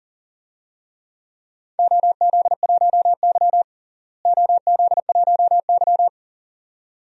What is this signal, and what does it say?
Morse code, heard as a keyed tone, about 40 wpm, 700 Hz. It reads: OZ1Y OZ1Y
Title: DXCC-100 Call Signs - 4 Characters - Repeated 2x - 40wpm